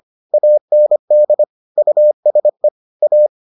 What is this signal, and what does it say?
Morse code, heard as a keyed tone, about 25 wpm, 600 Hz. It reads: AND USE A